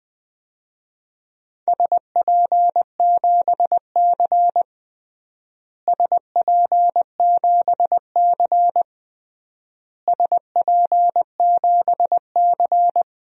Morse code, heard as a keyed tone, about 20 wpm, 700 Hz. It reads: SP7C SP7C SP7C
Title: DXCC-100 Call Signs - 4 Characters - Repeated 3x - 20wpm